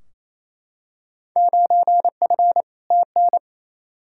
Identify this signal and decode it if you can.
Morse code, heard as a keyed tone, about 28 wpm, 700 Hz. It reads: 9F TN
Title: Field Day Exchanges Encore - 28wpm